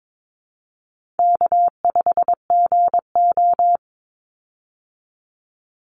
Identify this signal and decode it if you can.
Morse code, heard as a keyed tone, about 22 wpm, 700 Hz. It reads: K5GO
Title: Call Signs Repeated 2x - 22wpm